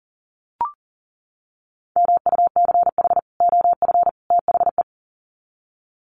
Morse code, heard as a keyed tone, about 40 wpm, 700 Hz. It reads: MUCH OF THE
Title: Sets of 3 Words 40wpm